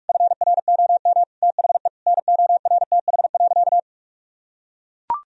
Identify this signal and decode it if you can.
Morse code, heard as a keyed tone, about 42 wpm, 700 Hz. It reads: FROM THE NORTH.
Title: Sentences from Top 1000 Words - Review Part 2 - 45wpm